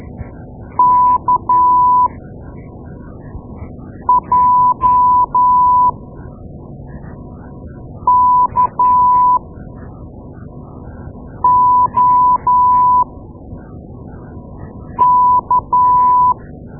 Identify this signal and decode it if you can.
Morse code, heard as a keyed tone, about 12 wpm, spaced as for 4 wpm, 990 Hz. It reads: KJKOK
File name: Morse Lessons / Morse Lessons / Tape-2A